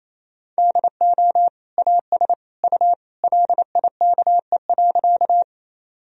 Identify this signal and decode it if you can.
Morse code, heard as a keyed tone, about 28 wpm, 700 Hz. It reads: DO AS U LIKE.
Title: Sentences from Top 100 Words and Abbreviations 28wpm